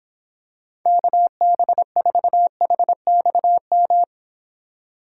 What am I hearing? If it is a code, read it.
Morse code, heard as a keyed tone, about 25 wpm, 700 Hz. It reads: KB4HXM